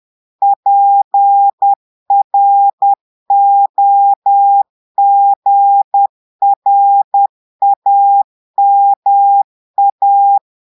Morse code, heard as a keyed tone, about 10 wpm, 800 Hz. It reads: PROGRAMA
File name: portugais texte en clair 2 vitesse 10 mots minutes